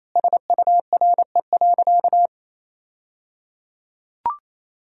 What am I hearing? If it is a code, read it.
Morse code, heard as a keyed tone, about 28 wpm, 700 Hz. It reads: SURE.
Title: 3x Word-Spacing - Sentences from Top 500 Words - 28wpm